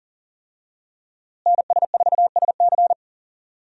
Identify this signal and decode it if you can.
Morse code, heard as a keyed tone, about 37 wpm, 700 Hz. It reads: NS4SC
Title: USA Call Signs - 5 Characters - Repeated 2x - 40wpm